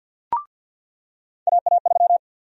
Morse code, heard as a keyed tone, about 52 wpm, 700 Hz. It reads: AA3
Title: USA Call Sign Prefixes - 50wpm